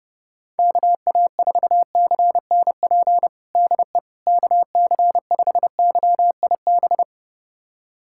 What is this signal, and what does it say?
Morse code, heard as a keyed tone, about 30 wpm, 700 Hz. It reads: KA4CNP DE KC5YIB
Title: USA Callsign DE Callsign - 6 Characters - 30wpm